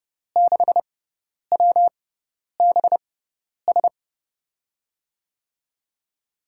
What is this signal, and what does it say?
Morse code, heard as a keyed tone, about 30 wpm, spaced as for 10 wpm, 700 Hz. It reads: 6WBS
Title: USA Call Signs - 6 Characters - 6x Character Spacing - 30wpm